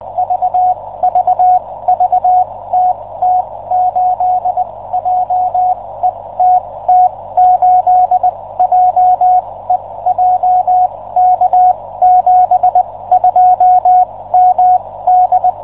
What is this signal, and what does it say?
Morse code, heard as a keyed tone, about 20 wpm, spaced as for 16 wpm, 730 Hz. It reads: VVV TT8JE TT8JE JK72MD